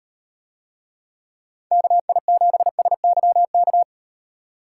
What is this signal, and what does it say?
Morse code, heard as a keyed tone, about 38 wpm, 700 Hz. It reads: KI7SYK